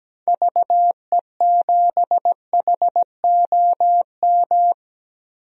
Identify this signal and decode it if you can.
Morse code, heard as a keyed tone, about 17 wpm, 700 Hz. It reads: VE7HOM
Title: DXCC-100 Call Signs - 6 Characters - Repeated 3x - 17wpm